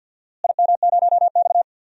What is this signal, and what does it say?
Morse code, heard as a keyed tone, about 52 wpm, 700 Hz. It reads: IM0X
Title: DXCC-100 Call Signs - 4 Characters - Repeated 2x - 50wpm